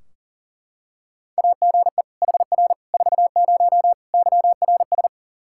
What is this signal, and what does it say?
Morse code, heard as a keyed tone, about 37 wpm, 700 Hz. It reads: AGE HR 40 YRS
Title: Practice QSO - Exchange 3 - More Info - 40wpm